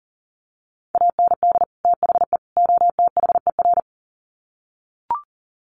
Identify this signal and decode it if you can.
Morse code, heard as a keyed tone, about 40 wpm, 700 Hz. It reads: AND THE OTHER
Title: Sets of 3 Words 40wpm